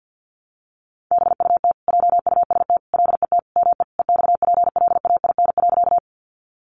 Morse code, heard as a keyed tone, about 50 wpm, 700 Hz. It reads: BUT JUST LET ME EXPLAIN.
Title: Sentences from Top 1100 Words 50wpm